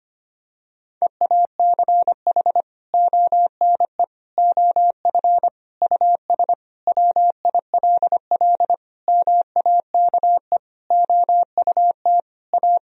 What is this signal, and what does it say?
Morse code, heard as a keyed tone, about 25 wpm, 700 Hz. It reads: EACH ONE OF US WILL MAKE OUT A
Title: Sentences from Top 1000 Words - Review Part 2 - 25wpm